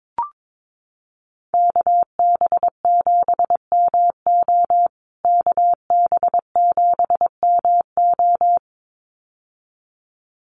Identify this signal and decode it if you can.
Morse code, heard as a keyed tone, about 22 wpm, 700 Hz. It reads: KB7MO KB7MO
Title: USA Call Signs Repeated 2x 22wpm